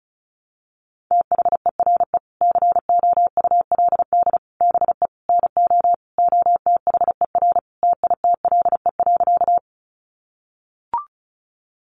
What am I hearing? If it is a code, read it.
Morse code, heard as a keyed tone, about 34 wpm, 700 Hz. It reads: THERE COULD BE NO OTHER TITLE.
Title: Sentences from Top 1200 Words 35wpm